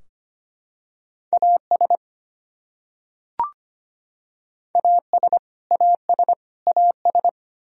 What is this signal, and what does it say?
Morse code, heard as a keyed tone, about 25 wpm, 700 Hz. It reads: AS AS AS AS
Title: Top 100 Words Repeated 25wpm